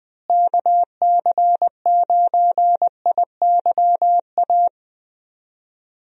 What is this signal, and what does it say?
Morse code, heard as a keyed tone, about 20 wpm, 700 Hz. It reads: KC9IYA
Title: USA Call Signs - 6 Characters - Repeated 3x - 20wpm